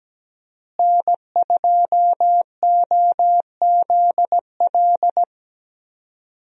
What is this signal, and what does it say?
Morse code, heard as a keyed tone, about 17 wpm, 700 Hz. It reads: N2OZL
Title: USA Call Signs - 5 Characters - Repeated 3x - 17wpm